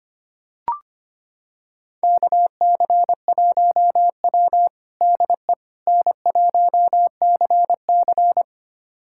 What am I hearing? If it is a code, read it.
Morse code, heard as a keyed tone, about 25 wpm, 700 Hz. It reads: KC1W DE N1CC
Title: USA Callsign DE Callsign - 4 Characters - 25wpm